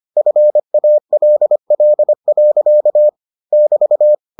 Morse code, heard as a keyed tone, about 25 wpm, 600 Hz. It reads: FALL. <BT>